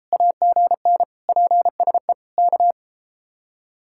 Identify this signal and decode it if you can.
Morse code, heard as a keyed tone, about 32 wpm, 700 Hz. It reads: AGN PSE K